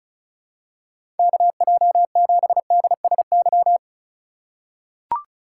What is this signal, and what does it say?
Morse code, heard as a keyed tone, about 36 wpm, 700 Hz. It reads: KJ7DSY
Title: USA Call Signs - 2x Character Spacing - 35wpm